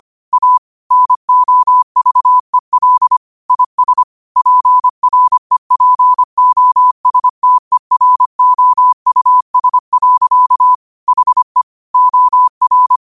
Morse code, heard as a keyed tone, about 25 wpm, 1000 Hz. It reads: A NOVEL IS PREPOSTEROUS. HE OR